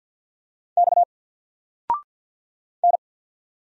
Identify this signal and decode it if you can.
Morse code, heard as a keyed tone, about 53 wpm, 700 Hz. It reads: X N
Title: Single Letter-Number 50wpm